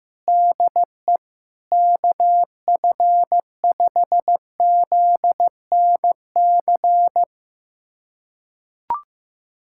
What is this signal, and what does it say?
Morse code, heard as a keyed tone, about 15 wpm, 700 Hz. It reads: DE KF5ZNC
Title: USA Callsign DE Callsign - 6 Characters - 15wpm